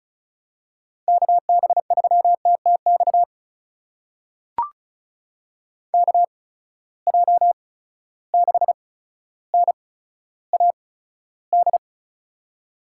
Morse code, heard as a keyed tone, about 36 wpm, spaced as for 15 wpm, 700 Hz. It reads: KB3TTX KJ6NAD
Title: USA Call Signs - 6 Characters - 8x Character Spacing - 35wpm